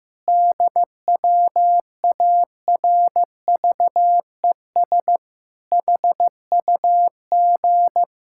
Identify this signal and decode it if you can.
Morse code, heard as a keyed tone, about 15 wpm, 700 Hz. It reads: DWARVES HUG
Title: Pangram Challenge v2 - 15wpm